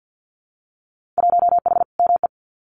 Morse code, heard as a keyed tone, about 50 wpm, 700 Hz. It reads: 1H ME
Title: Winter Field Day Exchanges Encore 50wpm